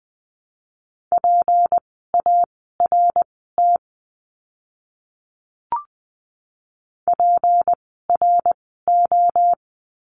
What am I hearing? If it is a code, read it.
Morse code, heard as a keyed tone, about 20 wpm, spaced as for 15 wpm, 700 Hz. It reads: PART PRO
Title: 153 Lesson 25f - Letter p - Words 20wpm